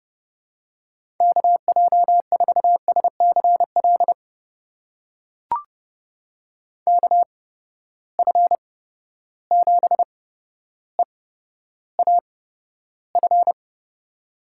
Morse code, heard as a keyed tone, about 30 wpm, spaced as for 13 wpm, 700 Hz. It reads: KJ4SCL KF7EAF
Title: USA Call Signs - 6 Characters - 8x Character Spacing - 30wpm